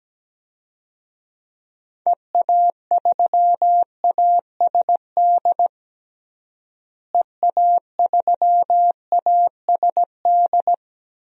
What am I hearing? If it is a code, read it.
Morse code, heard as a keyed tone, about 17 wpm, 700 Hz. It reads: EA3ASD EA3ASD
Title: DXCC-100 Call Signs - 6 Characters - Repeated 3x - 17wpm